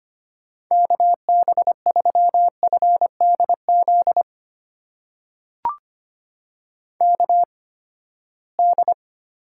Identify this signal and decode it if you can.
Morse code, heard as a keyed tone, about 25 wpm, spaced as for 17 wpm, 700 Hz. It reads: KB3FDZ KD
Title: USA Call Signs - 6 Characters - 8x Character Spacing - 25wpm